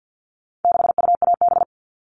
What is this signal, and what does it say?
Morse code, heard as a keyed tone, about 50 wpm, 700 Hz. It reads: 6UAB
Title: All Cut Numbers - 4-Digit 50wpm